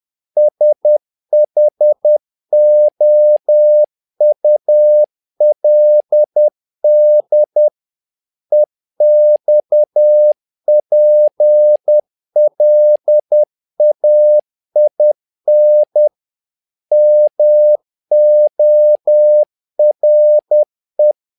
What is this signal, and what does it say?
Morse code, heard as a keyed tone, about 10 wpm, 600 Hz. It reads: SHOULD EXPLAIN MORE
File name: learning center / tower trouble / tower trouble 10wpm